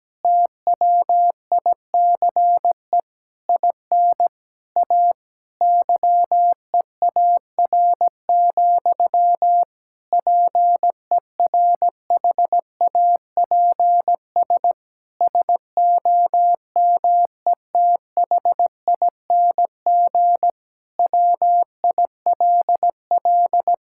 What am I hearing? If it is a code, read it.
Morse code, heard as a keyed tone, about 17 wpm, 700 Hz. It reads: TWICE IN A YEAR, PERHAPS SOMETHING WILL
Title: Sentences from Top 1600 Words 17wpm